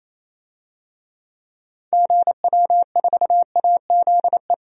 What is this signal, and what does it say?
Morse code, heard as a keyed tone, about 28 wpm, 700 Hz. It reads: GW4AZE
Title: DXCC-100 Call Signs - 6 Characters - 28wpm